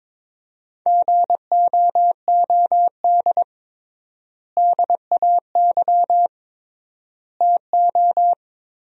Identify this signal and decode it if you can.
Morse code, heard as a keyed tone, about 23 wpm, 700 Hz. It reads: GOOD DAY TO
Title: 3x Word-Spacing - Sentences from Top 200 Words - 22wpm